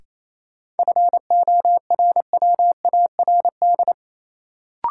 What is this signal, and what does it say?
Morse code, heard as a keyed tone, about 28 wpm, 700 Hz. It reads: FORWARD
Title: Sentences from Top 1300 Words - New Word Drill - 28wpm